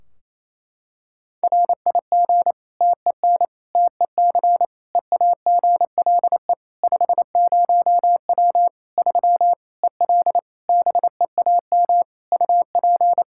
Morse code, heard as a keyed tone, about 28 wpm, 700 Hz. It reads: RIG TEN TEC EAGLE 50W 3 EL BEAM UP
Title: Practice QSO - Exchange 2 - Info - 28wpm